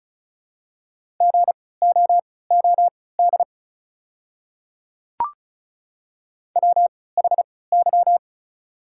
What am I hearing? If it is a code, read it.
Morse code, heard as a keyed tone, about 34 wpm, spaced as for 20 wpm, 690 Hz. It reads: GOOD WHY
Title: 3x Character Spacing - Top 200 Words - 35wpm